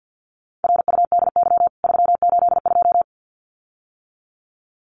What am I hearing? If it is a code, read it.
Morse code, heard as a keyed tone, about 50 wpm, 700 Hz. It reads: RUDY 382
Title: CWT Exchange 50wpm